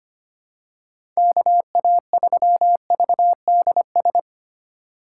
Morse code, heard as a keyed tone, about 25 wpm, 700 Hz. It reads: KA3VDS
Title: USA Call Signs - 6 Characters - 4x Character Spacing - 25wpm